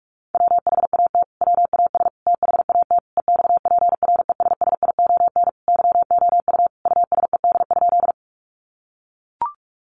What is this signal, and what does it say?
Morse code, heard as a keyed tone, about 44 wpm, 700 Hz. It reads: WHAT WAS THAT EXPRESSION YOU USED?
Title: Sentences from Top 1200 Words 45wpm